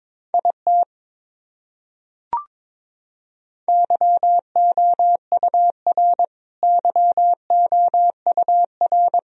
Morse code, heard as a keyed tone, about 22 wpm, 700 Hz. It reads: IT YOUR YOUR
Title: Top 100 Words Repeated 22wpm